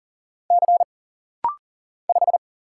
Morse code, heard as a keyed tone, about 40 wpm, 700 Hz. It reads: C 5
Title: Single Letter-Number Alt - Rapid-Fire 40wpm